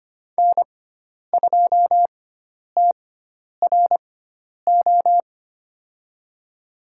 Morse code, heard as a keyed tone, about 25 wpm, spaced as for 10 wpm, 700 Hz. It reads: N2TRO
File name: Lesson 16g - Letter c - Call Signs 25wpm